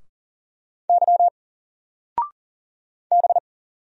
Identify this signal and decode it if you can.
Morse code, heard as a keyed tone, about 39 wpm, 700 Hz. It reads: Y B
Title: Single Letter-Number 40wpm